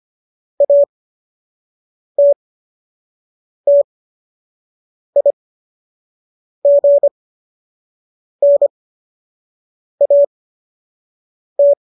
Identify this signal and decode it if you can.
Morse code, heard as a keyed tone, about 25 wpm, spaced as for 6 wpm, 570 Hz. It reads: ATTIGNAT